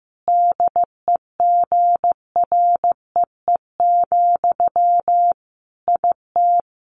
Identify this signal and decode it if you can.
Morse code, heard as a keyed tone, about 14 wpm, 700 Hz. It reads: DEGREE, IT
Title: Sentences from Top 1500 Words 15wpm